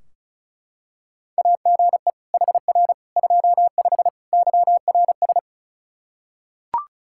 Morse code, heard as a keyed tone, about 34 wpm, 700 Hz. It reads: AGE HR 25 YRS
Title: K1USN SST QSO Practice 35wpm